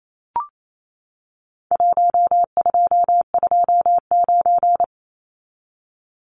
Morse code, heard as a keyed tone, about 28 wpm, 700 Hz. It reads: 1229
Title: Numbers - Contest Style - 4-Digit 28wpm